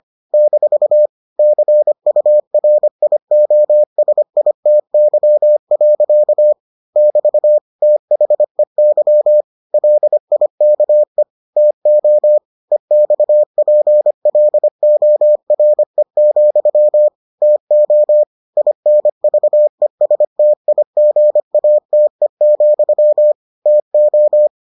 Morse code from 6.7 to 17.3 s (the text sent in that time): <BT> THEY LIKE TO EXPLORE,